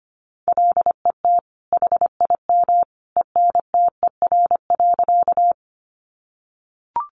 Morse code, heard as a keyed tone, about 25 wpm, 700 Hz. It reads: LET HIM ENTER.
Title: Sentences from Top 1300 Words 25wpm